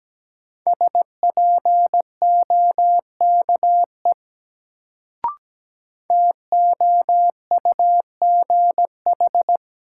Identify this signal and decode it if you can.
Morse code, heard as a keyed tone, about 17 wpm, 700 Hz. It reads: SPOKE TOUGH
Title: Sentences from Top 1200 Words - New Word Drill - 17wpm